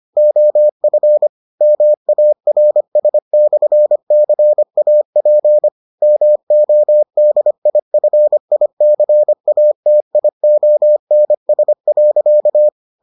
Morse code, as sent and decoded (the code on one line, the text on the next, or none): --- ..-. / -- .- .-. ... -..-. -.-. .- .--. / -- --- -.. .. ..-. .. -.-. .- - .. --- -. ... .-.-.-
OF MARS/CAP MODIFICATIONS.